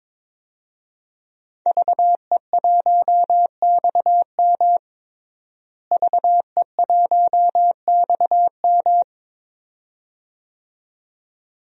Morse code, heard as a keyed tone, about 22 wpm, 700 Hz. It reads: VE1XM VE1XM
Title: DXCC-100 Call Signs - 5 Characters - Repeated 2x - 22wpm